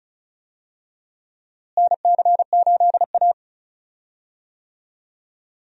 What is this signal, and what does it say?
Morse code, heard as a keyed tone, about 34 wpm, 700 Hz. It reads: NC8A
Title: USA Call Signs - 4 Characters - Repeated 3x - 35wpm